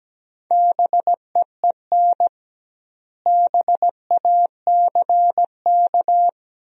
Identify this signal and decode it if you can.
Morse code, heard as a keyed tone, about 17 wpm, 700 Hz. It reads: BEEN BACK
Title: 2x Word-Spacing - Sentences from Top 200 Words - 17wpm